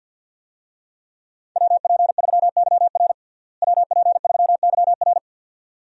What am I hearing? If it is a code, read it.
Morse code, heard as a keyed tone, about 52 wpm, 700 Hz. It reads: WP3YR WP3YR
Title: USA Call Signs - 5 Characters - Repeated 2x - 50wpm